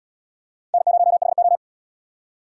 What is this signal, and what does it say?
Morse code, heard as a keyed tone, about 77 wpm, 700 Hz. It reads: N0HO